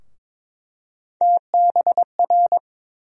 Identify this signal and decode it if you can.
Morse code, heard as a keyed tone, about 21 wpm, 700 Hz. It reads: TBR
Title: USA Call Sign Postfixes - 3 Letters 22wpm